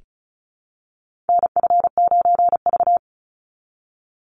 Morse code, heard as a keyed tone, about 34 wpm, 700 Hz. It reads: NF9V
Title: Call Signs Repeated 2x - 35wpm